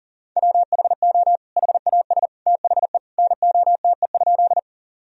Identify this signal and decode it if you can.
Morse code, heard as a keyed tone, about 40 wpm, 700 Hz. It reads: WHO HAS THE NOTE?
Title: Sentences from Top 1000 Words - Review Part 2 - 40wpm